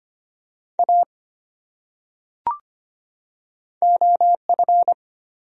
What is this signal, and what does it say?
Morse code, heard as a keyed tone, about 25 wpm, 700 Hz. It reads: A OF
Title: Top 100 Words 25wpm